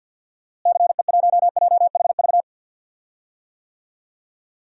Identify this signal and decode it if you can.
Morse code, heard as a keyed tone, about 52 wpm, 700 Hz. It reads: CE1JHV
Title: DXCC-100 Call Signs - 6 Characters - Repeated 3x - 50wpm